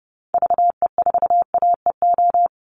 Morse code, heard as a keyed tone, about 30 wpm, 700 Hz. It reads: VE4AEO
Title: Call Signs Repeated 2x - 30wpm